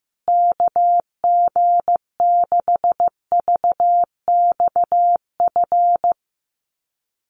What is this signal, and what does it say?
Morse code, heard as a keyed tone, about 15 wpm, 700 Hz. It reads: KG6VXF
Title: USA Call Signs Repeated 2x 15wpm